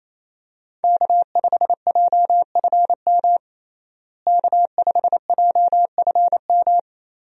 Morse code, heard as a keyed tone, about 28 wpm, 700 Hz. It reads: K5JFM K5JFM
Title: USA Call Signs - 5 Characters - Repeated 3x - 28wpm